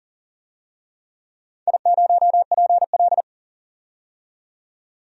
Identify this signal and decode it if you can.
Morse code, heard as a keyed tone, about 40 wpm, 700 Hz. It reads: I0PL
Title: DXCC-100 Call Signs - 4 Characters - 40wpm